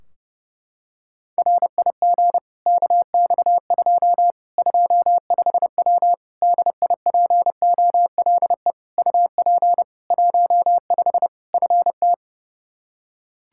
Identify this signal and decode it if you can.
Morse code, heard as a keyed tone, about 30 wpm, 700 Hz. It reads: RIG KX2 25W DIPOLE UP 15 FT
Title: Practice QSO - Exchange 2 - Info - 30wpm